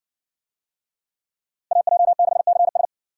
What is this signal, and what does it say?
Morse code, heard as a keyed tone, about 58 wpm, 700 Hz. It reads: AJ6CS